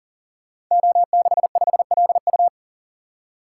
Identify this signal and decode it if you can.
Morse code, heard as a keyed tone, about 41 wpm, 700 Hz. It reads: O65LU
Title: Code Groups - 5 Letter:Numbers - 40wpm